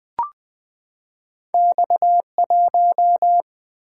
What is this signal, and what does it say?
Morse code, heard as a keyed tone, about 20 wpm, 700 Hz. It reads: X1
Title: Single-Letter Single-Number 20wpm